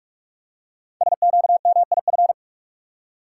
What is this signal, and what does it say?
Morse code, heard as a keyed tone, about 44 wpm, 700 Hz. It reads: SQMIF
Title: Code Groups - 5 Letters - 45wpm